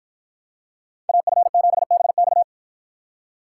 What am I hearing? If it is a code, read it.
Morse code, heard as a keyed tone, about 50 wpm, 700 Hz. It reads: AF7BX